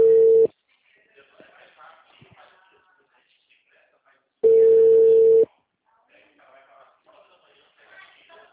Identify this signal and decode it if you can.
Ringback tone (China, 450 Hz), with speech.